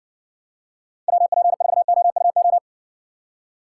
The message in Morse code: WP4OUO

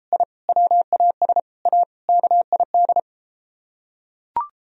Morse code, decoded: I WAS A KID

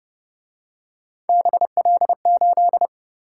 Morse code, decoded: BL8

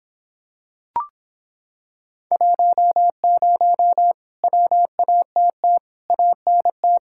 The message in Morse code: 10 WATT ANT

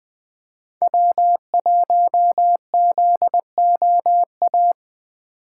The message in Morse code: W1ZOA

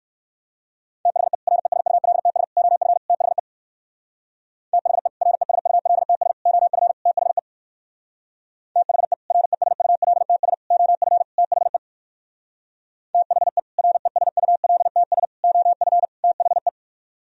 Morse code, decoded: THE RESULTS OF THE THE RESULTS OF THE THE RESULTS OF THE THE RESULTS OF THE

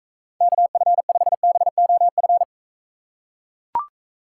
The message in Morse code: KF5BOF

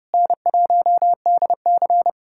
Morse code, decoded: N1DC

